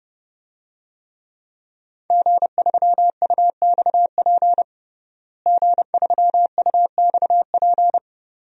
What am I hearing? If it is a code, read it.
Morse code, heard as a keyed tone, about 30 wpm, 700 Hz. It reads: G3UXP G3UXP
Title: DXCC-100 Call Signs - 5 Characters - Repeated 2x - 30wpm